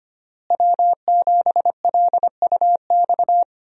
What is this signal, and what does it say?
Morse code, heard as a keyed tone, about 25 wpm, 700 Hz. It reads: W7LUX
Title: USA Call Signs - 5 Characters - Repeated 3x - 25wpm